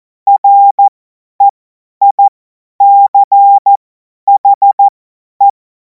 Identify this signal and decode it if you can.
Morse code, heard as a keyed tone, about 14 wpm, spaced as for 10 wpm, 800 Hz. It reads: REICHE
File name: Morsekurs / Witze / Witz 28. April 2026 50 F70